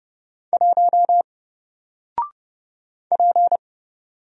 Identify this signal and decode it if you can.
Morse code, heard as a keyed tone, about 30 wpm, 700 Hz. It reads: 1 P
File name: Single-Letter-Number-30wpm